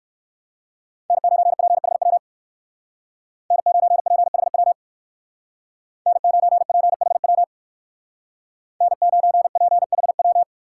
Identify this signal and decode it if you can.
Morse code, heard as a keyed tone, about 58 wpm, 700 Hz. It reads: N9PHW N9PHW N9PHW N9PHW